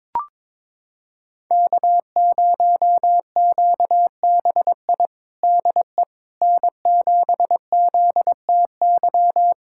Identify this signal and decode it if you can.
Morse code, heard as a keyed tone, about 21 wpm, 700 Hz. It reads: K0QBI DE N7ZTY
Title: USA Callsign DE Callsign - 5 Characters - 22wpm